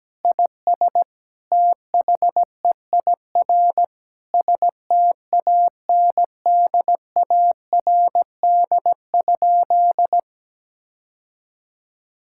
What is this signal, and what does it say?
Morse code, heard as a keyed tone, about 17 wpm, 700 Hz. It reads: IS THEIR STANDARD?
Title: Sentences from Top 1000 Words - Review Part 6 - 17wpm